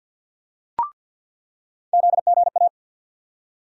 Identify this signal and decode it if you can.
Morse code, heard as a keyed tone, about 51 wpm, 700 Hz. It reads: ZGA
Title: USA Call Sign Postfixes - 3 Letters 50wpm